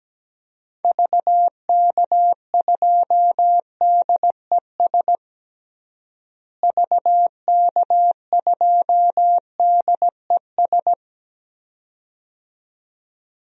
Morse code, heard as a keyed tone, about 17 wpm, 700 Hz. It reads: VK2DES VK2DES
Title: DXCC-100 Call Signs - 6 Characters - Repeated 2x - 17wpm